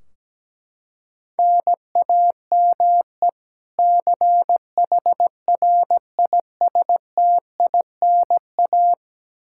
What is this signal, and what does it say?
Morse code, heard as a keyed tone, about 17 wpm, 700 Hz. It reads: NAME CHRISTINA
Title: Practice QSO - Exchange 1 - Beginning - 17wpm